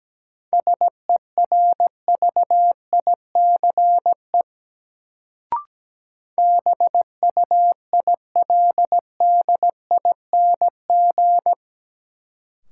Morse code, heard as a keyed tone, about 17 wpm, 700 Hz. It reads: SERVICE BUILDING
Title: Sentences from Top 700 Words - New Word Drill - 17wpm